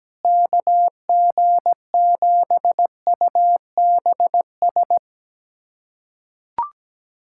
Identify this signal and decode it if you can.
Morse code, heard as a keyed tone, about 17 wpm, 700 Hz. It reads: KG7UBS
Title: USA Callsign DE Callsign 17wpm